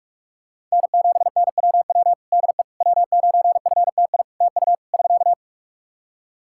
Morse code, heard as a keyed tone, about 44 wpm, 700 Hz. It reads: N7NWW DE W9FTI TU <SK>
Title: Practice QSO - Exchange 4 - Ending - 45wpm